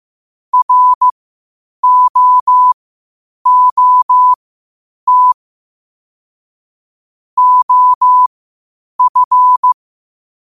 Morse code, heard as a keyed tone, about 15 wpm, spaced as for 8 wpm, 1000 Hz. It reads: ROOT OF
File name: qotdfiles / 8wpm